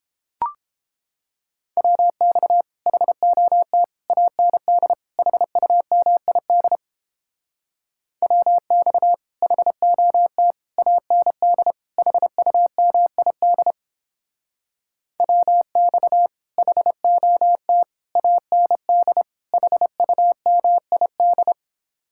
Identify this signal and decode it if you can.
Morse code, heard as a keyed tone, about 29 wpm, 700 Hz. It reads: WX HOT AND HUMID WX HOT AND HUMID WX HOT AND HUMID